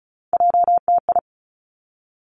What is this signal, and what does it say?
Morse code, heard as a keyed tone, about 36 wpm, 700 Hz. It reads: JTI
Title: Three Challenging Letter-Number Combinations 35wpm